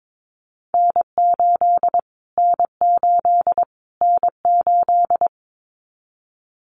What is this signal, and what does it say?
Morse code, heard as a keyed tone, about 22 wpm, 700 Hz. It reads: N8 N8 N8
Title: Callsign Prefixes for Top 25 DXCC - Repeated 3x - 22wpm